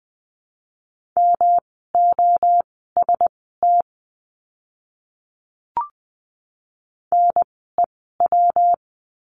Morse code, heard as a keyed tone, about 20 wpm, spaced as for 14 wpm, 700 Hz. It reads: MOST NEW
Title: 139 Lesson 23f - Letter f - Words 20wpm